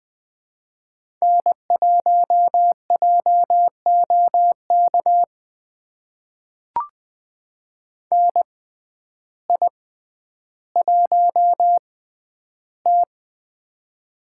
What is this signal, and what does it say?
Morse code, heard as a keyed tone, about 20 wpm, spaced as for 11 wpm, 700 Hz. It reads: N1JOK NI1T
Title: USA Call Signs - 6x Character Spacing - 20wpm